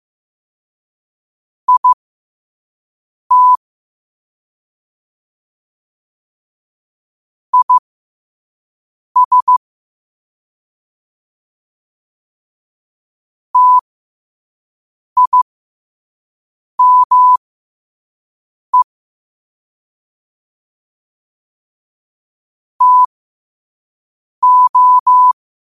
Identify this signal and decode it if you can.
Morse code, heard as a keyed tone, about 15 wpm, spaced as for 5 wpm, 1000 Hz. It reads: IT IS TIME TO
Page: New quotes every day in morse code at 5 Words per minute.